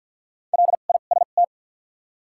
Morse code, heard as a keyed tone, about 53 wpm, spaced as for 37 wpm, 700 Hz. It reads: LIST
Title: 2x Character Spacing - Top 500 Words - 50wpm